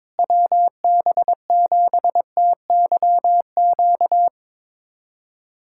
Morse code, heard as a keyed tone, about 22 wpm, 700 Hz. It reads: WB7TYQ
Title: USA Callsign DE Callsign - 6 Characters - 22wpm